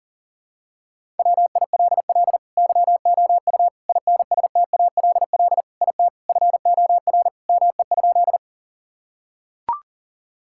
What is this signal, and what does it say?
Morse code, heard as a keyed tone, about 40 wpm, 700 Hz. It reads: WILL YOU INSTALL IT FOR ME?
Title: Sentences from Top 1600 Words 40wpm